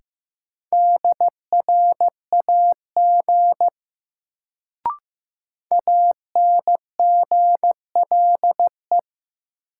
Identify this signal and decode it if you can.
Morse code, heard as a keyed tone, about 15 wpm, 700 Hz. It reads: DRAG ANGLE
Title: Sentences from Top 1500 Words - New Word Drill Alt - 15wpm